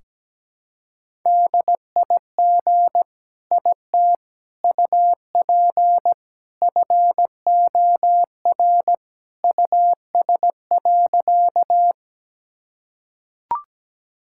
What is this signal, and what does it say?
Morse code, heard as a keyed tone, about 17 wpm, 700 Hz. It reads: DIG IT UP FOR US.
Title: Sentences from Top 1600 Words 17wpm